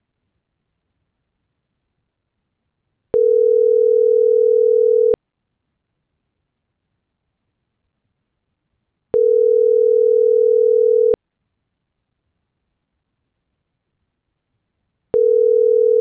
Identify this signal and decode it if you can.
Ringback tone (North America), 440 + 480 Hz.
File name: CENTRAL MABE / out saby.reyes25 - .97083